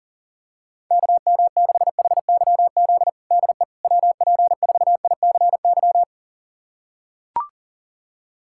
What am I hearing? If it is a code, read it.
Morse code, heard as a keyed tone, about 40 wpm, 700 Hz. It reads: KM6HYZ DE WP4ICY
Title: USA Callsign DE Callsign - 6 Characters - 40wpm